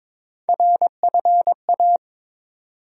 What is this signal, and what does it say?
Morse code, heard as a keyed tone, about 22 wpm, 700 Hz. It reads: RFA
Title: USA Call Sign Postfixes - 3 Letters 22wpm